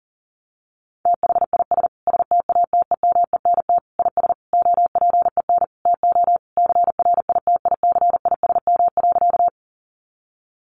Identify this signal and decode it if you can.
Morse code, heard as a keyed tone, about 40 wpm, 700 Hz. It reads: THIS STATEMENT IS OPEN TO CRITICISM.
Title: Sentences from Top 1300 Words 40wpm